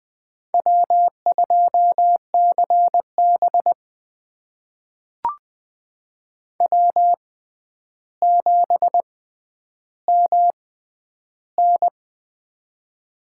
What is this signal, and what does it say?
Morse code, heard as a keyed tone, about 20 wpm, spaced as for 10 wpm, 700 Hz. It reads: W2CB W7MN
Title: USA Call Signs - 6x Character Spacing - 20wpm